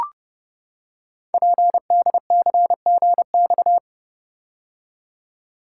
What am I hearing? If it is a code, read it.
Morse code, heard as a keyed tone, about 31 wpm, 700 Hz. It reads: PDCGX
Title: Code Groups - 5 Letters - 30wpm